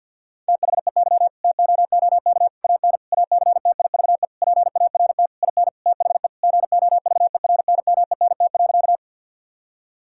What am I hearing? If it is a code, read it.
Morse code, heard as a keyed tone, about 50 wpm, 700 Hz. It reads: THEY TOOK AN ACTIVE PART IN THE GOVERNMENT.
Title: Sentences from Top 1000 Words - Review Part 6 - 50wpm